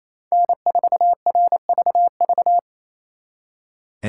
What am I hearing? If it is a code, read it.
Morse code, heard as a keyed tone, about 36 wpm, 700 Hz. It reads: N4RVV